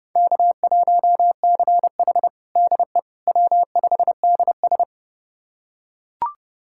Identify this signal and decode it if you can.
Morse code, heard as a keyed tone, about 32 wpm, 700 Hz. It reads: K1CH DE W5DS
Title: USA Callsign DE Callsign - 4 Characters - 30wpm